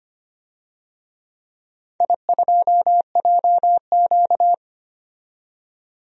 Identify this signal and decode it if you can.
Morse code, heard as a keyed tone, about 25 wpm, 700 Hz. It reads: I2JQ
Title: DXCC-100 Call Signs - 4 Characters - 25wpm